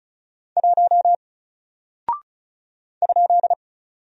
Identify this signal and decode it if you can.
Morse code, heard as a keyed tone, about 36 wpm, 700 Hz. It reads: 1 ?
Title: Single Character (Letter:Number:Punctuation:Prosign) Alt 35wpm